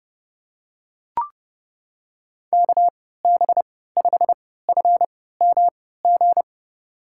Morse code, heard as a keyed tone, about 30 wpm, spaced as for 17 wpm, 700 Hz. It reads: KB5FMG
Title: USA Call Signs - 6 Characters - 3x Character Spacing - 30wpm